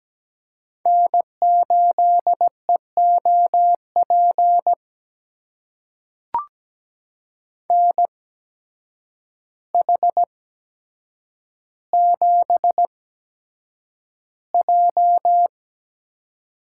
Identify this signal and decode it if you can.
Morse code, heard as a keyed tone, about 17 wpm, spaced as for 8 wpm, 700 Hz. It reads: N8EOP NH7J
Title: USA Call Signs - 5 Characters - 8x Character Spacing - 17wpm